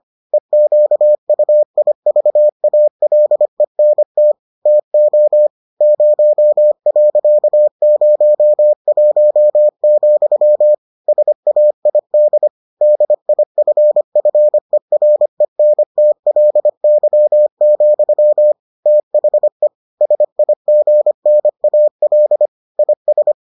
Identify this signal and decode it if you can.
Morse code, heard as a keyed tone, about 25 wpm, 600 Hz. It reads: EQUIVALENT TO 0.01, SAID DIFFERENTLY, THE SIGNAL IS